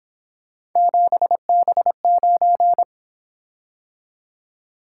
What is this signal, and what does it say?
Morse code, heard as a keyed tone, about 27 wpm, 700 Hz. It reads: 7B9
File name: Speed-Racing - Three Letter-Number Combinations - 22wpm